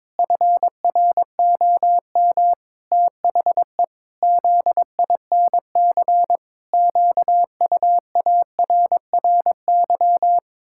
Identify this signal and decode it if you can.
Morse code, heard as a keyed tone, about 21 wpm, 700 Hz. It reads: FROM THE ZINC QUARRY
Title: Pangram Challenge v2 - 22wpm